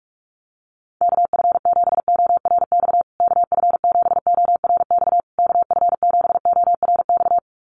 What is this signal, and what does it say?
Morse code, heard as a keyed tone, about 44 wpm, 700 Hz. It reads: KF7ORX KF7ORX KF7ORX
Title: USA Call Signs Repeated 3x 45wpm